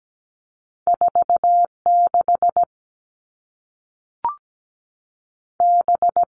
Morse code, heard as a keyed tone, about 17 wpm, 700 Hz. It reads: 46 B
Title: All Cut Numbers - 2-Digit 17wpm